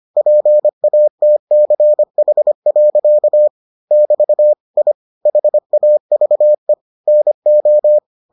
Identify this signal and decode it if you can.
Morse code, heard as a keyed tone, about 25 wpm, 600 Hz. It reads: PATCH. <BT> I HAVE NO